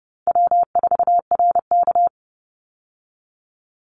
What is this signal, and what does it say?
Morse code, heard as a keyed tone, about 30 wpm, 700 Hz. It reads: W4RK
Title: Easy Call Signs Repeated 3x - 30wpm